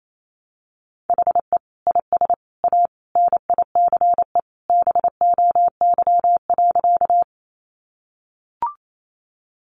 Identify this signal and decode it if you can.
Morse code, heard as a keyed tone, about 28 wpm, 700 Hz. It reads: HE IS A NICE BOY.
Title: Sentences from Top 1100 Words 28wpm